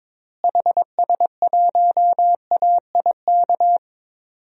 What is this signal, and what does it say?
Morse code, heard as a keyed tone, about 22 wpm, 700 Hz. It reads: HS1AIK
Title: DXCC-100 Call Signs - 6 Characters - Repeated 2x - 22wpm